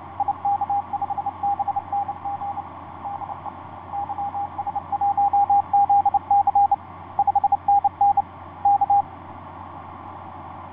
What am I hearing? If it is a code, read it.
Morse code, heard as a keyed tone, about 29 wpm, 810 Hz, with radio noise. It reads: IK5BNG DE YS1ZC 5NN K
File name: ys1zc ik2cik etc 30m cw 07feb2010 2249